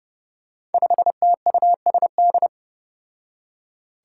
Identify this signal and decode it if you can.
Morse code, heard as a keyed tone, about 29 wpm, 700 Hz. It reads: 5TUSD
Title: Five Challenging Letter-Number Combinations 30wpm